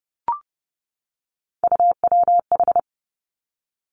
Morse code, heard as a keyed tone, about 31 wpm, 700 Hz. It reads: UWH
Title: Three Challenging Letter-Number Combinations 30wpm